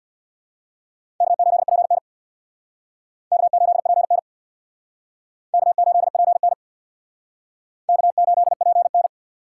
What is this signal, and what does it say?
Morse code, heard as a keyed tone, about 60 wpm, 690 Hz. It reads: K8PN K8PN K8PN K8PN
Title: Speed Racing - USA 1x2 Call Signs 50wpm